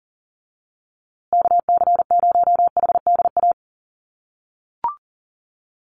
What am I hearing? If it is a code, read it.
Morse code, heard as a keyed tone, about 40 wpm, 700 Hz. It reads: KC0HDA
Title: USA Call Signs Repeated 2x 40wpm